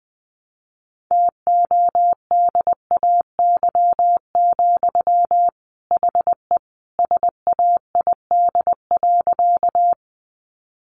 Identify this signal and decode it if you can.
Morse code, heard as a keyed tone, about 19 wpm, 700 Hz. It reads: TODAY, HE SAID.
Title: Sentences from Top 600 Words 20wpm